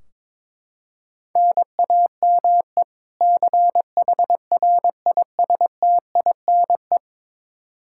Morse code, heard as a keyed tone, about 22 wpm, 700 Hz. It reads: NAME CHRISTINE
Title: Practice QSO - Exchange 1 - Beginning - 22wpm